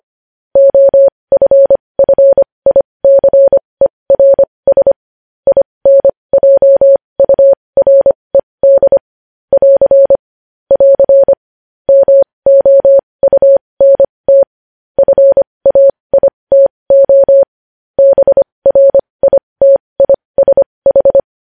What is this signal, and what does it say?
Morse code, heard as a keyed tone, about 25 wpm, spaced as for 20 wpm, 550 Hz. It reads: OFFICERS INJURED <AR> <AR> MOUNT FAITO BRITISH
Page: news-headlines-in-morse-code-at-20-wpm